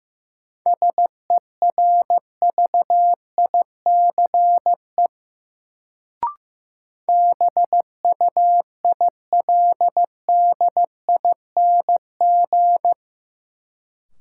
Morse code, heard as a keyed tone, about 15 wpm, 700 Hz. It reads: SERVICE BUILDING
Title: Sentences from Top 700 Words - New Word Drill - 15wpm